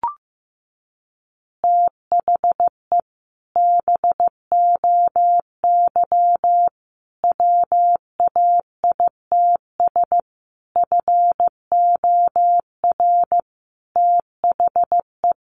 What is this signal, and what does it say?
Morse code, heard as a keyed tone, about 15 wpm, 700 Hz. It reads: THE BOY WAITS FOR THE
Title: Additional Easy Phrases 15wpm